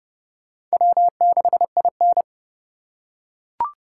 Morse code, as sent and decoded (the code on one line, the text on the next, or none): .-- -.... .. -.
W6IN